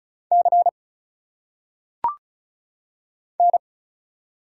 Morse code, heard as a keyed tone, about 36 wpm, 700 Hz. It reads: C N
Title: Single Letters - Phonetic Pronunciation 35wpm